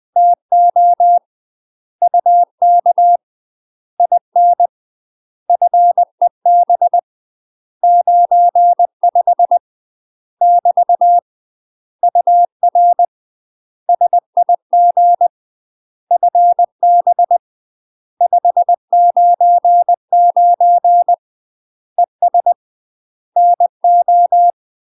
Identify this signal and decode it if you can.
Morse code, heard as a keyed tone, about 20 wpm, 700 Hz. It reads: TO UK IN FEB 95 <BT> UR SIG FB 599 ES NO
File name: UK MASTER QSOs TRAINING / 20WPM / UK MASTER QSOs 0029